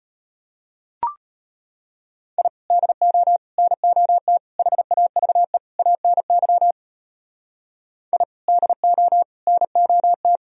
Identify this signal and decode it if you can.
Morse code, heard as a keyed tone, about 30 wpm, 700 Hz. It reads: I DO NOT HAVE ANY I DO NOT
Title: Speed-Racing - Sets of 5 Words 25wpm